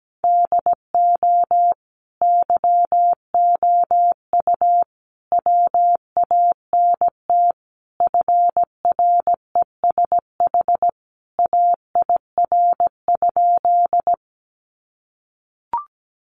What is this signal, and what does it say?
Morse code, heard as a keyed tone, about 17 wpm, 700 Hz. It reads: DO YOU WANT FRESH AIR?
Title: Sentences from Top 1200 Words 17wpm